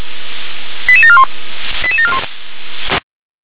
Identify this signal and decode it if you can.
Five-tone selective call (ZVEI), heard as quick runs of five tones, tones of about 70 ms. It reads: 88741 88741